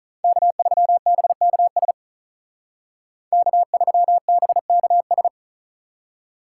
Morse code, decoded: K3BKS K3BKS

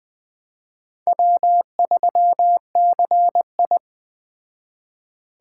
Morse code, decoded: W3CI

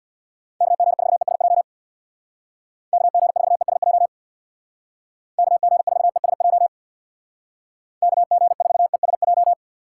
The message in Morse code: KG4ESJ KG4ESJ KG4ESJ KG4ESJ

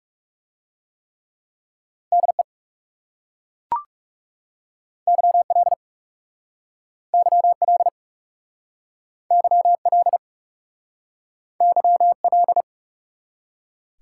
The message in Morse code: DE YL YL YL YL

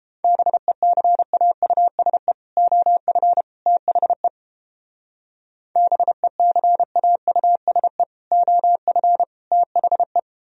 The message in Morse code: BECAUSE OF THE BECAUSE OF THE

